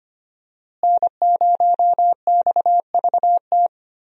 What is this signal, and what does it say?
Morse code, heard as a keyed tone, about 25 wpm, 700 Hz. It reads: N0XVT
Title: USA Call Signs - 5 Characters - Repeated 2x - 25wpm